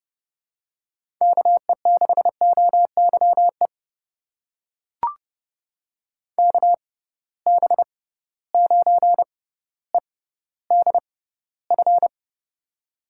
Morse code, heard as a keyed tone, about 32 wpm, spaced as for 16 wpm, 700 Hz. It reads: KE6OYE KB9EDF
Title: USA Call Signs - 6 Characters - 6x Character Spacing - 30wpm